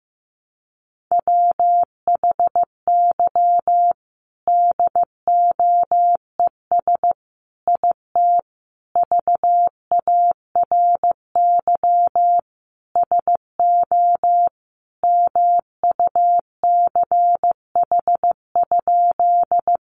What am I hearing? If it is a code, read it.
Morse code, heard as a keyed tone, about 15 wpm, 700 Hz. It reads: WHY DOES IT VARY SO MUCH?
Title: Sentences from Top 1200 Words 15wpm